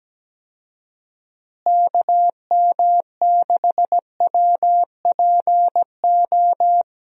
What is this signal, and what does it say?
Morse code, heard as a keyed tone, about 17 wpm, 700 Hz. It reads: KM6WPO
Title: USA Call Signs - 6 Characters - Repeated 2x - 17wpm